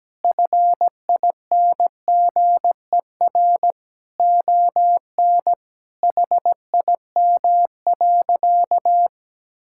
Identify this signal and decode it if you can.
Morse code, heard as a keyed tone, about 17 wpm, 700 Hz. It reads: FINGER ON HIM.
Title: Sentences from Top 1600 Words 17wpm